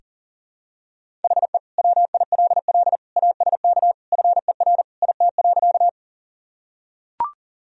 Morse code, heard as a keyed tone, about 40 wpm, 700 Hz. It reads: HE WILL ASK FER IT.
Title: Sentences from Top 200 Words and Abbreviations 40wpm